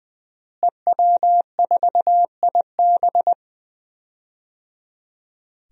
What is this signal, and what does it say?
Morse code, heard as a keyed tone, about 20 wpm, 700 Hz. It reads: EW4IB
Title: DXCC-100 Call Signs - 5 Characters - Repeated 3x - 20wpm